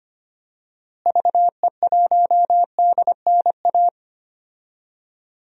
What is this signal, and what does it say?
Morse code, heard as a keyed tone, about 25 wpm, 700 Hz. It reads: VE1DNA
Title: DXCC-100 Call Signs - 6 Characters - Repeated 3x - 25wpm